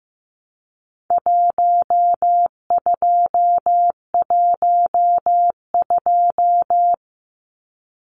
Morse code, heard as a keyed tone, about 15 wpm, 700 Hz. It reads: 1212
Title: Numbers - Contest Style - 4-Digit 15wpm